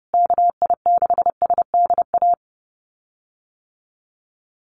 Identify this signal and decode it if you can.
Morse code, heard as a keyed tone, about 30 wpm, 700 Hz. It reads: KI6SDA
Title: USA Call Signs Repeated 3x 30wpm